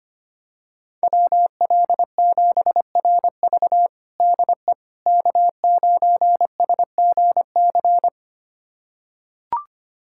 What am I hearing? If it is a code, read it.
Morse code, heard as a keyed tone, about 25 wpm, 700 Hz. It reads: WL7RV DE K9SGC
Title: USA Callsign DE Callsign - 5 Characters - 25wpm